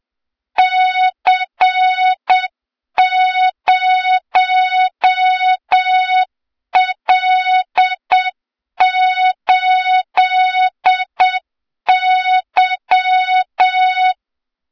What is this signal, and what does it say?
Morse code, heard as a keyed tone, about 7 wpm, 3700 Hz, with noise. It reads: C0L8Y
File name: Morze-HA2MN / hang / 23 GYAKORLAT